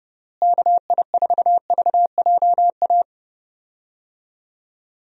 Morse code, heard as a keyed tone, about 32 wpm, 700 Hz. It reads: KI4VJA
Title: USA Call Signs - 6 Characters - Repeated 2x - 30wpm